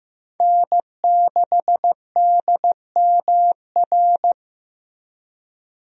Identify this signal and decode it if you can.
Morse code, heard as a keyed tone, about 15 wpm, 700 Hz. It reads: N6DMR
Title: USA Call Signs - 5 Characters - Repeated 3x - 15wpm